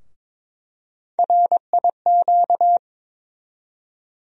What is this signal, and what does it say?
Morse code, heard as a keyed tone, about 22 wpm, 700 Hz. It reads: RIQ
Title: USA Call Sign Postfixes - 3 Letters 22wpm